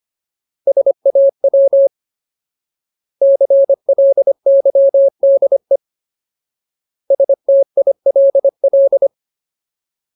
Morse code, heard as a keyed tone, about 25 wpm, 550 Hz. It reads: SAW CLYDE STILL
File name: Ultimate Word List / 25WPM Ultimate